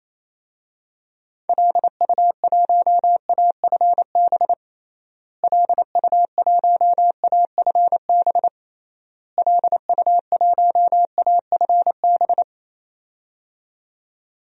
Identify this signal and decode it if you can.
Morse code, heard as a keyed tone, about 28 wpm, 700 Hz. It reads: LU1AFB LU1AFB LU1AFB
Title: DXCC-100 Call Signs - 6 Characters - Repeated 3x - 28wpm